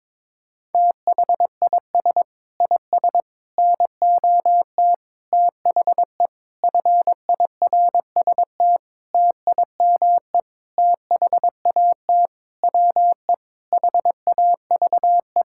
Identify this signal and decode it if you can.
Morse code, heard as a keyed tone, about 22 wpm, 700 Hz. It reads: THIS IS NOT THE FIRST TIME THAT WE HAVE
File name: Sentences with 7-Word Phrases - 22wpm